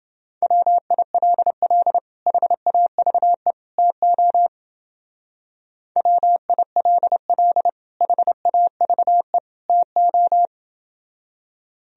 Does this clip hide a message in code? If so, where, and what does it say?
Yes: Morse code from 0.4 to 10.5 s, heard as a keyed tone, about 28 wpm, 700 Hz. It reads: WILL HAVE TO WILL HAVE TO